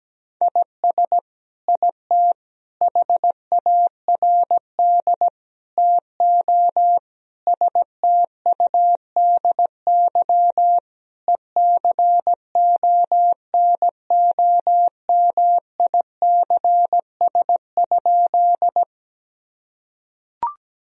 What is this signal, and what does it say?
Morse code, heard as a keyed tone, about 17 wpm, 700 Hz. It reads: IS IT HARD TO STUDY ECONOMICS?
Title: Sentences from Top 1000 Words - Review Part 3 - 17wpm